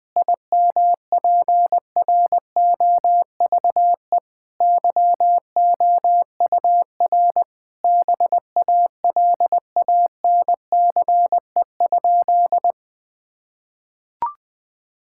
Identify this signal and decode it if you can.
Morse code, heard as a keyed tone, about 21 wpm, 700 Hz. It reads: IMPROVE YOUR BALANCE?
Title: Sentences from Top 1000 Words - Review Part 6 - 20wpm